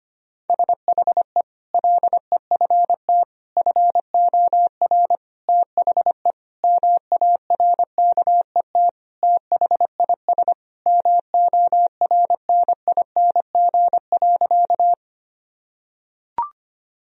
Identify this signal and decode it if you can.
Morse code, heard as a keyed tone, about 25 wpm, 700 Hz. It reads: SHE LEFT FOR THE MARKET THIS MORNING.
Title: Sentences from Top 1000 Words - Review Part 3 - 25wpm